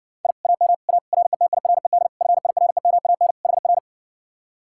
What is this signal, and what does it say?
Morse code, heard as a keyed tone, about 69 wpm, 700 Hz. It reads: I AM A RETIRED FIREMAN HR